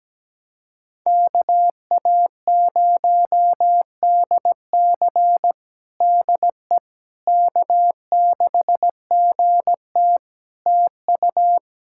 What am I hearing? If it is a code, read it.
Morse code, heard as a keyed tone, about 17 wpm, 700 Hz. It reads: KA0DC DE K6GT TU
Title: Practice QSO - Exchange 4 - Ending - 17wpm